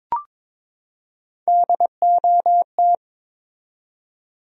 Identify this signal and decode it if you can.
Morse code, heard as a keyed tone, about 22 wpm, 700 Hz. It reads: DOT
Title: USA Common Names - Max 4 Letters - 22wpm